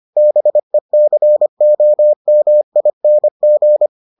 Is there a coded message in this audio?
Yes: Morse code from 0.2 to 3.9 s, heard as a keyed tone, about 25 wpm, 600 Hz. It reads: BECOMING